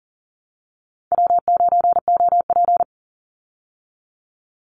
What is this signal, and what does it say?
Morse code, heard as a keyed tone, about 40 wpm, 700 Hz. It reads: W9OP
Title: Call Signs Repeated 2x - 40wpm